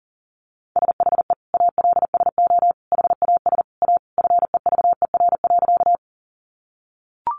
Morse code, heard as a keyed tone, about 40 wpm, 700 Hz. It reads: SHE ALSO HAS A FEVER.
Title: Easy Phrases 40wpm